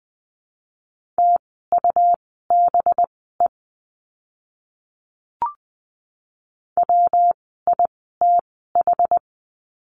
Morse code, heard as a keyed tone, about 20 wpm, spaced as for 14 wpm, 700 Hz. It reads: TUBE WITH
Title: 185 Lesson 30f - Letter b - Words 20wpm